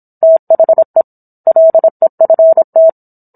Morse code, heard as a keyed tone, about 27 wpm, 650 Hz. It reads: THE LEFT